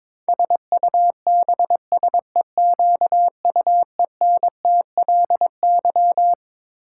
Morse code, heard as a keyed tone, about 22 wpm, 700 Hz. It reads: SUBSEQUENTLY
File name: Twelve Letter Words - 22wpm